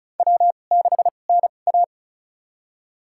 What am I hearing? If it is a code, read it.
Morse code, heard as a keyed tone, about 36 wpm, spaced as for 25 wpm, 700 Hz. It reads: W6NA
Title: USA Call Signs - 4 Characters - 2x Character Spacing - 35wpm